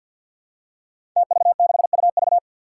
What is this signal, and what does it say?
Morse code, heard as a keyed tone, about 51 wpm, 700 Hz. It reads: TV6UV
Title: Five Challenging Letter-Number Combinations 50wpm